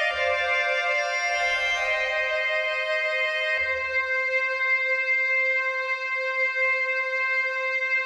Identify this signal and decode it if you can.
Dial-up modem handshake.